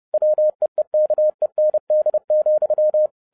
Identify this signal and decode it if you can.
Morse code, heard as a keyed tone, about 30 wpm, 610 Hz. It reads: WEEKEND,
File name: 2012-01-mp3 / n12-I DIDNT MISS THE SOLAR CYCLE 24-30wpm